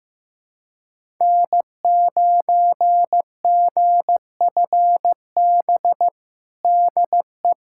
Morse code, heard as a keyed tone, about 15 wpm, 700 Hz. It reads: N9GFB DE
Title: USA Callsign DE Callsign - 5 Characters - 15wpm